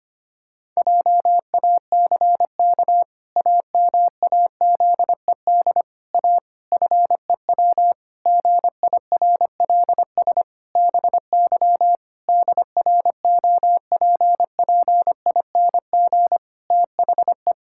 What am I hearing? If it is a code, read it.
Morse code, heard as a keyed tone, about 25 wpm, 700 Hz. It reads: JACK AMAZED A FEW GIRLS BY DROPPING THE
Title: Pangram Challenge v2 - 25wpm